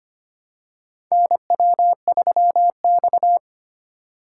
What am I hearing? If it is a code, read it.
Morse code, heard as a keyed tone, about 25 wpm, 700 Hz. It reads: NW3X
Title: USA Call Signs - 4x Character Spacing - 25wpm